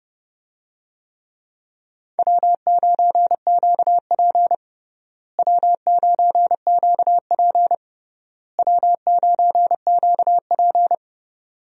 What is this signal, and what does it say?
Morse code, heard as a keyed tone, about 30 wpm, 700 Hz. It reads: W9QP W9QP W9QP
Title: USA Call Signs - 4 Characters - Repeated 3x - 30wpm